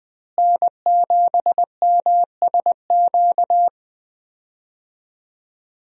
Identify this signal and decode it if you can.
Morse code, heard as a keyed tone, about 20 wpm, 700 Hz. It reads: N7MSQ
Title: USA Call Signs - 5 Characters - Repeated 3x - 20wpm